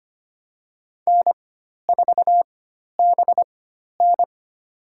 Morse code, heard as a keyed tone, about 25 wpm, spaced as for 12 wpm, 700 Hz. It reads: N4BN
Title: USA Call Signs - 4 Characters - 4x Character Spacing - 25wpm